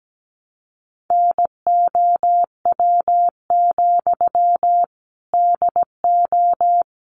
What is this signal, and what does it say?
Morse code, heard as a keyed tone, about 17 wpm, 700 Hz. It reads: NOW, DO
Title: Sentences from Top 700 Words 17wpm